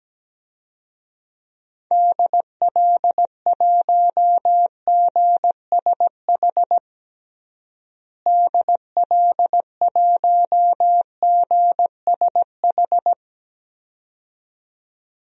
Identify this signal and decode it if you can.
Morse code, heard as a keyed tone, about 17 wpm, 700 Hz. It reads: DL1GSH DL1GSH
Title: DXCC-100 Call Signs - 6 Characters - Repeated 2x - 17wpm